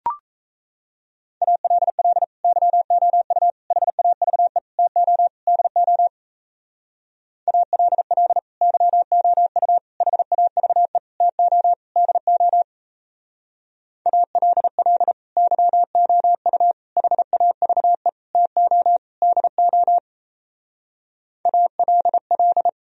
Morse code, heard as a keyed tone, about 34 wpm, 700 Hz. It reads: ALL YOU HAVE TO DO ALL YOU HAVE TO DO ALL YOU HAVE TO DO ALL
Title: Speed-Racing - Sets of 5 Words 28wpm